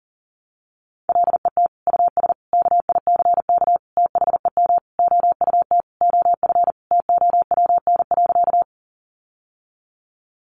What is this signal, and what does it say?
Morse code, heard as a keyed tone, about 37 wpm, 700 Hz. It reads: LET US KICK THEM OUT OF TOWN.
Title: Sentences from Top 1300 Words 40wpm